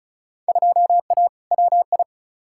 Morse code, heard as a keyed tone, about 36 wpm, 700 Hz. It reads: 2A WI
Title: Field Day Exchanges Encore - 35wpm